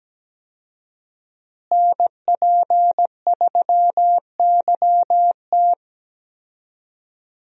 Morse code, heard as a keyed tone, about 17 wpm, 700 Hz. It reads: NP3YT
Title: USA Call Signs - 5 Characters - Repeated 2x - 17wpm